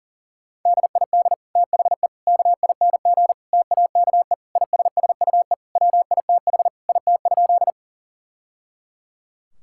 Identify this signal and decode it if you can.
Morse code, heard as a keyed tone, about 40 wpm, 700 Hz. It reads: DID THE KING TAKE ISSUE WITH IT?
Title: Sentences from Top 1000 Words - Review Part 6 - 40wpm